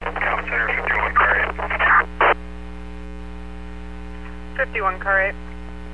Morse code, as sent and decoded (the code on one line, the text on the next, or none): none